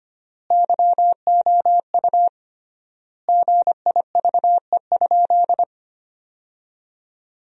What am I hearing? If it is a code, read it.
Morse code, heard as a keyed tone, about 25 wpm, 700 Hz. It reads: YOU GIVE?
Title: 3x Word-Spacing - Sentences from Top 200 Words - 25wpm